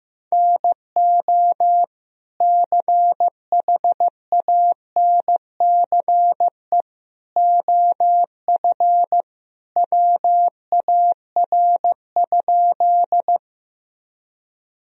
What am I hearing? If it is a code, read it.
Morse code, heard as a keyed tone, about 15 wpm, 700 Hz. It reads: NO CHANCE OF WAR?
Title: Sentences from Top 1000 Words - Review Part 2 - 15wpm